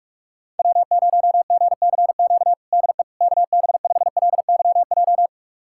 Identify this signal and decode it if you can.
Morse code, heard as a keyed tone, about 46 wpm, 700 Hz. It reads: W0GCQ DE KB5LYJ
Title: USA Callsign DE Callsign 45wpm